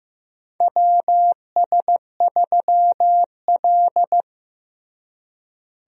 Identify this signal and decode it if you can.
Morse code, heard as a keyed tone, about 15 wpm, 700 Hz. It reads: WS3L
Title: USA Call Signs - 4 Characters - Repeated 2x - 15wpm